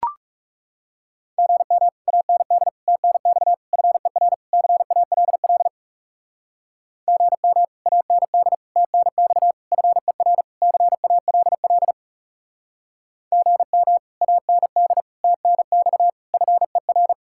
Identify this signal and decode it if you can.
Morse code, heard as a keyed tone, about 36 wpm, 700 Hz. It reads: GM AND TNX FER CALL GM AND TNX FER CALL GM AND TNX FER
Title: Speed-Racing Common QSO Phrases Encore - 30wpm